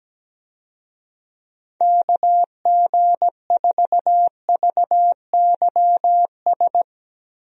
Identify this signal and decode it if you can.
Morse code, heard as a keyed tone, about 17 wpm, 700 Hz. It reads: KG4VYS
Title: USA Call Signs - 6 Characters - Repeated 2x - 17wpm